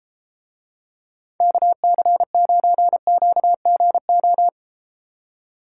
Morse code, heard as a keyed tone, about 32 wpm, 700 Hz. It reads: KC9QGO